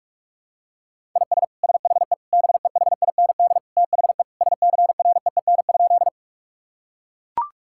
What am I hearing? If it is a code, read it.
Morse code, heard as a keyed tone, about 41 wpm, 700 Hz. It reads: IS SHE BEHIND THE SCREEN?
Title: Sentences from Top 1000 Words - Review Part 6 - 45wpm